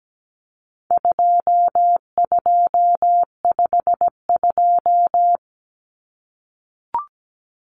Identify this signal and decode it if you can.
Morse code, heard as a keyed tone, about 17 wpm, 700 Hz. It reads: 2252
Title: Common Cut Numbers - 4-Digit 17wpm